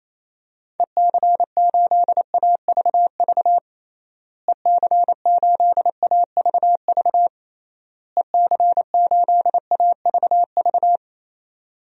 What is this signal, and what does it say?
Morse code, heard as a keyed tone, about 28 wpm, 700 Hz. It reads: EC8AVV EC8AVV EC8AVV
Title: DXCC-100 Call Signs - 6 Characters - Repeated 3x - 28wpm